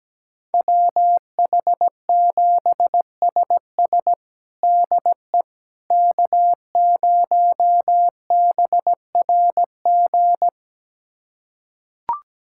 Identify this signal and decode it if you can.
Morse code, heard as a keyed tone, about 17 wpm, 700 Hz. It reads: WH7SS DE K0BRG
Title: USA Callsign DE Callsign - 5 Characters - 17wpm